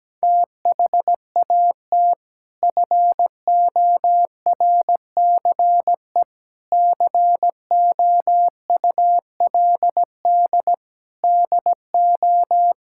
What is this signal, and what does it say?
Morse code, heard as a keyed tone, about 17 wpm, 700 Hz. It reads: THAT FORCE COULD DO
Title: Sentences from Top 1000 Words - Review Part 5 - 17wpm